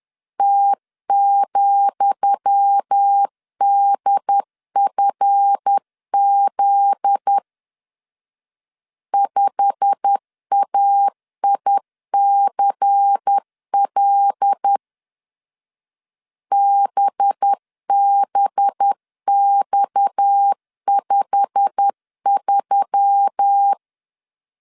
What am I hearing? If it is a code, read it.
Morse code, heard as a keyed tone, about 10 wpm, 800 Hz. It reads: T,DFZ 5AICL BBX53
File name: LEZIONI / 10 SIGNS / 10D COMA 10wpm 3013